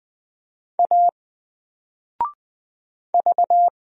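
Morse code, heard as a keyed tone, about 20 wpm, 700 Hz. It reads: A V
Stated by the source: Single letter and number Morse drill at 20 wpm, from N4LCD.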